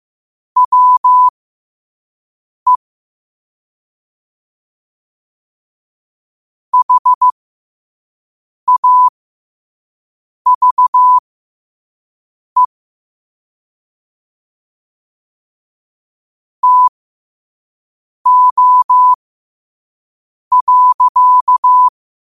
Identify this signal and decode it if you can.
Morse code, heard as a keyed tone, about 15 wpm, spaced as for 5 wpm, 1000 Hz. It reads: WE HAVE TO.
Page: 5 WPM morse code quotes